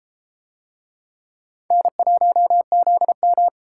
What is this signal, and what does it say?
Morse code, heard as a keyed tone, about 33 wpm, 700 Hz. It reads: N1ZM